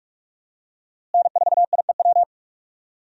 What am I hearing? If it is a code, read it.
Morse code, heard as a keyed tone, about 47 wpm, 700 Hz. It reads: N4IEW
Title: USA Call Signs - 5 Characters - Repeated 2x - 45wpm